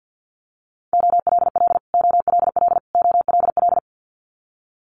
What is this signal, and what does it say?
Morse code, heard as a keyed tone, about 50 wpm, 700 Hz. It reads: OLL OLL OLL
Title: Multi-letter Phonemes Repeated 3x 50wpm